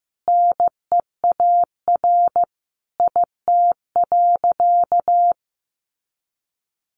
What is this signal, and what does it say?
Morse code, heard as a keyed tone, about 15 wpm, 700 Hz. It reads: NEAR IT.
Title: A Sound Of Thunder 15wpm